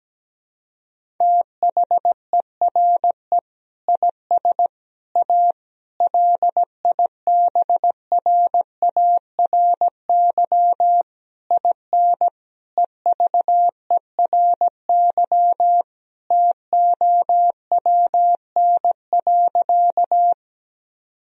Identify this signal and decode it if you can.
Morse code, heard as a keyed tone, about 17 wpm, 700 Hz. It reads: THERE IS A LIBRARY IN EVERY TOWN.
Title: Sentences from Top 1000 Words - Review Part 4 - 17wpm